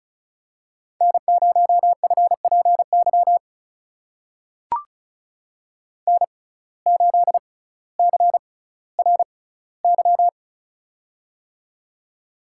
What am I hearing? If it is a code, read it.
Morse code, heard as a keyed tone, about 35 wpm, spaced as for 18 wpm, 700 Hz. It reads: N0FPY N8CRY
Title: USA Call Signs - 5 Characters - 6x Character Spacing - 35wpm